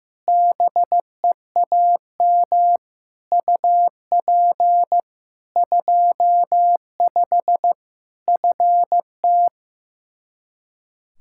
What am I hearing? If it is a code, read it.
Morse code, heard as a keyed tone, about 15 wpm, 700 Hz. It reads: BEAM UP 25 FT
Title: Practice QSO - Exchange 2 - Info - 15wpm